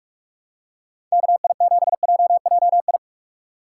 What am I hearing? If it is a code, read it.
Morse code, heard as a keyed tone, about 42 wpm, 700 Hz. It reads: KI7JJI
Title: USA Call Signs - 6 Characters - 2x Character Spacing - 45wpm